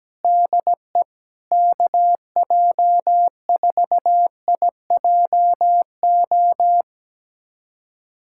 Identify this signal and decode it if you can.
Morse code, heard as a keyed tone, about 17 wpm, 700 Hz. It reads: DE KJ4IJO
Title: USA Callsign DE Callsign - 6 Characters - 17wpm